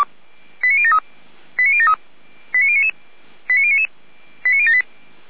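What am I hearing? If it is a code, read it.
Five-tone selective call (ZVEI), heard as quick runs of five tones, tones of about 70 ms. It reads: 89073 89073 89099 89099 89078
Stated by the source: Five-tone ZVEI selective call 89168, received on SDR.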